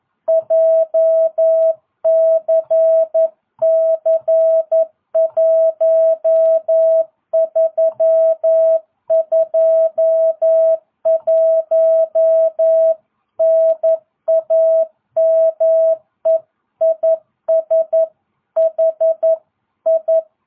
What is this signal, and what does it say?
Morse code, heard as a keyed tone, about 11 wpm, 650 Hz. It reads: JCC1321 NAME IS HI